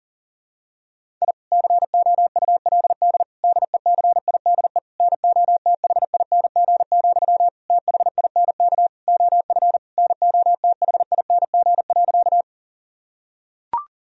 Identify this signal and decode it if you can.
Morse code, heard as a keyed tone, about 40 wpm, 700 Hz. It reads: I COULD DECIDE NOTHING, THINK OF NOTHING.
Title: Sentences from Top 1000 Words - Review Part 3 - 40wpm